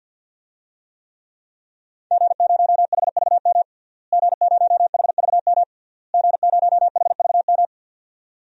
Morse code, heard as a keyed tone, about 50 wpm, 700 Hz. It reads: G0HVM G0HVM G0HVM
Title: DXCC-100 Call Signs - 5 Characters - Repeated 3x - 50wpm